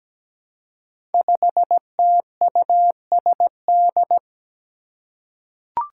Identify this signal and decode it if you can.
Morse code, heard as a keyed tone, about 17 wpm, 700 Hz. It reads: 5TUSD
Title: Five Challenging Letter-Number Combinations 17wpm